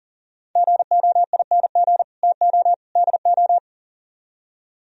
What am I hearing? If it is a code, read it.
Morse code, heard as a keyed tone, about 50 wpm, 700 Hz. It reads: GOING TO DO
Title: Speed-Racing - Sets of 3 Words 40wpm